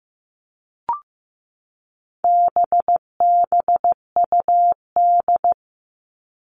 Morse code, heard as a keyed tone, about 15 wpm, 700 Hz. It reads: BBUD
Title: All Cut Numbers - 4-Digit 15wpm